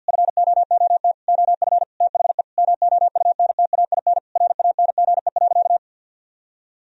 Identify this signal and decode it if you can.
Morse code, heard as a keyed tone, about 50 wpm, 700 Hz. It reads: FOOT OF THE MOUNTAIN RANGE.
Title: Sentences from Top 1000 Words - Review Part 4 - 50wpm